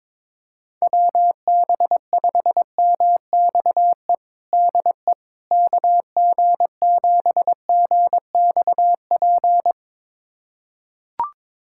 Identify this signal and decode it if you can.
Morse code, heard as a keyed tone, about 22 wpm, 700 Hz. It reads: WB5MXE DE KG7GXP
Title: USA Callsign DE Callsign - 6 Characters - 22wpm